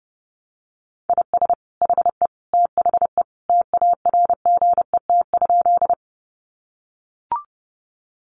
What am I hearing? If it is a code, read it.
Morse code, heard as a keyed tone, about 30 wpm, 700 Hz. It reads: IS HE THE TARGET?
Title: Sentences from Top 1200 Words 30wpm